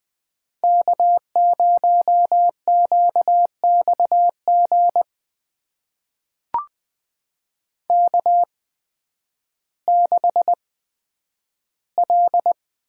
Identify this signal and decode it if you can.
Morse code, heard as a keyed tone, about 20 wpm, spaced as for 11 wpm, 700 Hz. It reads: K0QXG K6L
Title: USA Call Signs - 5 Characters - 8x Character Spacing - 20wpm